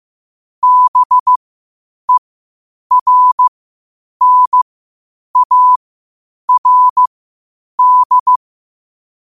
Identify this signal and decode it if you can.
Morse code, heard as a keyed tone, about 15 wpm, spaced as for 8 wpm, 1000 Hz. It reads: BERNARD
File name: qotdfiles / 8wpm